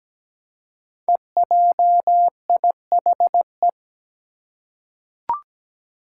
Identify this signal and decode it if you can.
Morse code, heard as a keyed tone, about 17 wpm, 700 Hz. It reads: EJIHE
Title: Five Challenging Letter-Number Combinations 17wpm